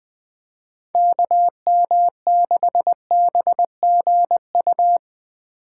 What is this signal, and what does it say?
Morse code, heard as a keyed tone, about 20 wpm, 700 Hz. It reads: KM6BGU
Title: USA Call Signs - 6 Characters - 4x Character Spacing - 20wpm